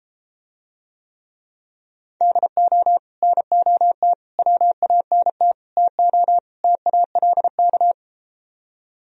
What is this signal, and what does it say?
Morse code, heard as a keyed tone, about 33 wpm, 700 Hz. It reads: DO NOT WANT TO TALK